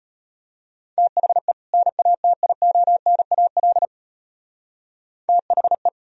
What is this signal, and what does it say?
Morse code, heard as a keyed tone, about 32 wpm, 700 Hz. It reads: THE NATIONAL THE